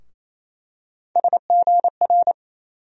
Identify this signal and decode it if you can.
Morse code, heard as a keyed tone, about 27 wpm, 700 Hz. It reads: SGR
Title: USA Call Sign Postfixes - 3 Letters 28wpm